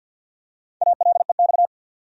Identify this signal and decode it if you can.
Morse code, heard as a keyed tone, about 52 wpm, 700 Hz. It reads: ALEX
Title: USA Common Names - Max 4 Letters - 50wpm